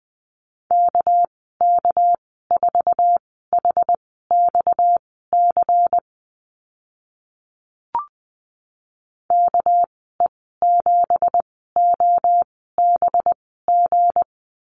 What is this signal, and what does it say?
Morse code, heard as a keyed tone, about 20 wpm, spaced as for 15 wpm, 700 Hz. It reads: KK4HXC KE7OBG
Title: 231 Lesson 37g - Letter x - Call Signs 20wpm